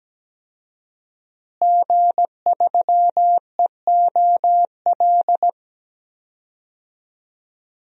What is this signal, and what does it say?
Morse code, heard as a keyed tone, about 17 wpm, 700 Hz. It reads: G3EOL
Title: DXCC-100 Call Signs - 5 Characters - Repeated 3x - 17wpm